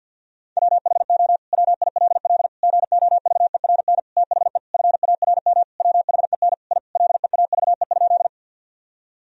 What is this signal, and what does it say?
Morse code, heard as a keyed tone, about 50 wpm, 700 Hz. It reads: WHO WILL GOVERN THE FARM WHEN I LEAVE?
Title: Sentences from Top 1000 Words - Review Part 2 - 50wpm